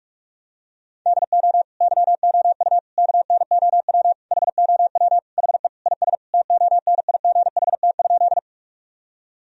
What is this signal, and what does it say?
Morse code, heard as a keyed tone, about 41 wpm, 700 Hz. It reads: DO YOU KNOW HOW HE IS TONIGHT?
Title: Sentences from Top 1000 Words - Review Part 3 - 45wpm